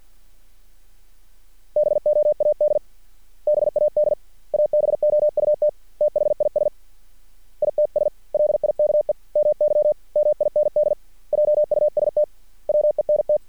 Morse code, heard as a keyed tone, about 50 wpm, 600 Hz. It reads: BOAD BAD ABOUT THIS ITS LIKE MY MIND JUST WENT
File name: archived / CRSnet